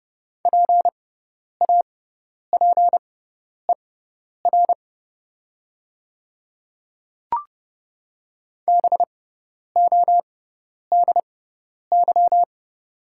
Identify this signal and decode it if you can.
Morse code, heard as a keyed tone, about 30 wpm, spaced as for 10 wpm, 700 Hz. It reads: PAPER BODY
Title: 6x Character Spacing - Top 500 Words - 30wpm